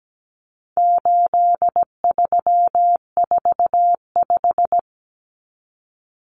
Morse code, heard as a keyed tone, about 17 wpm, 700 Hz. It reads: 8345
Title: Common Cut Numbers - 4-Digit 17wpm